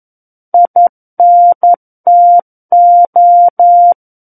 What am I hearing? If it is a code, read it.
Morse code, heard as a keyed tone, about 11 wpm, 700 Hz. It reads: INTO
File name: Book One / 02-Chapter 2 - 11 WPM